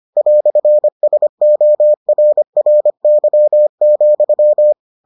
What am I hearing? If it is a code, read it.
Morse code, heard as a keyed tone, about 25 wpm, 600 Hz. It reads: "SORRY,